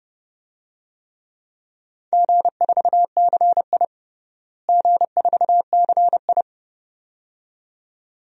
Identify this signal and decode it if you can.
Morse code, heard as a keyed tone, about 30 wpm, 700 Hz. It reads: G4CI G4CI
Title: DXCC-100 Call Signs - 4 Characters - Repeated 2x - 30wpm